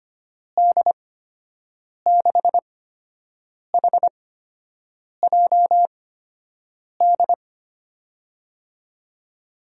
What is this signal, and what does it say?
Morse code, heard as a keyed tone, about 25 wpm, spaced as for 7 wpm, 700 Hz. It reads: D6HJD
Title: USA Call Signs - 8x Character Spacing - 25wpm